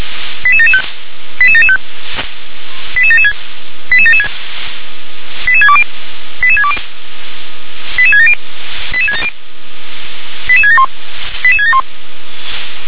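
Five-tone selective call (ZVEI), heard as quick runs of five tones, tones of about 70 ms. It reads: 88775 88775 88776 88776 88620 88620 88670 88671 88671
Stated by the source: ZVEI five-tone selective call received on SDR.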